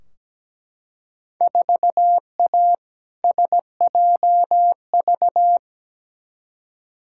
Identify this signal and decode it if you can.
Morse code, heard as a keyed tone, about 17 wpm, 700 Hz. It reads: 4A SJV
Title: Field Day Exchanges Encore - 17wpm